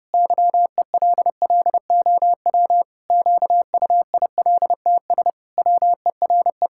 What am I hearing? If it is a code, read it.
Morse code, heard as a keyed tone, about 32 wpm, 700 Hz. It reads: YELLOW QUILTS WERE
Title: Pangram Challenge v2 - 30wpm